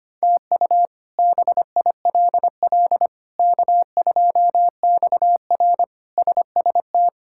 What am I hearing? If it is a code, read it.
Morse code, heard as a keyed tone, about 25 wpm, 700 Hz. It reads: TU BILL K2XR SST
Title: K1USN SST QSO Practice - 25wpm